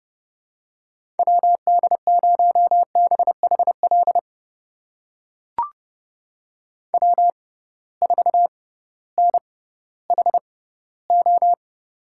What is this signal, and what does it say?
Morse code, heard as a keyed tone, about 31 wpm, spaced as for 16 wpm, 700 Hz. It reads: WD0BHL W4NHO
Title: USA Call Signs - 6x Character Spacing - 30wpm